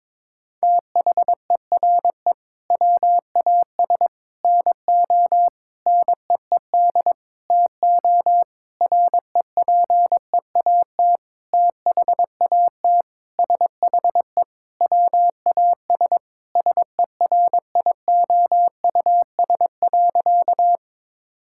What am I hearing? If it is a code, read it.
Morse code, heard as a keyed tone, about 22 wpm, 700 Hz. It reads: THERE WAS NO NEED TO REPEAT THAT SHE WAS SERIOUS.
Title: Sentences from Top 1600 Words 22wpm